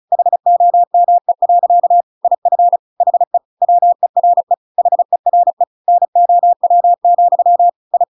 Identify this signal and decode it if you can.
Morse code, heard as a keyed tone, about 35 wpm, 700 Hz. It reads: HOME. IF HE WERE HERE NOW, I